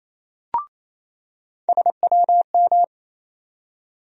Morse code, heard as a keyed tone, about 28 wpm, 700 Hz. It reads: SWM
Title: USA Call Sign Postfixes - 3 Letters 28wpm